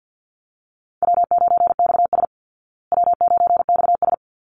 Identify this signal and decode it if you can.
Morse code, heard as a keyed tone, about 50 wpm, 700 Hz. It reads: W9XS W9XS
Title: Easy Call Signs Repeated 2x - 50wpm